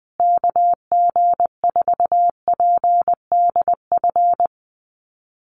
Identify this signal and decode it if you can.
Morse code, heard as a keyed tone, about 20 wpm, 700 Hz. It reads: KG4PDF
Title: USA Call Signs Repeated 3x 20wpm